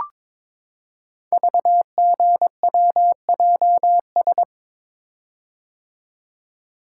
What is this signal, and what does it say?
Morse code, heard as a keyed tone, about 22 wpm, 700 Hz. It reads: VGWJS
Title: Code Groups - 5 Letters - 22wpm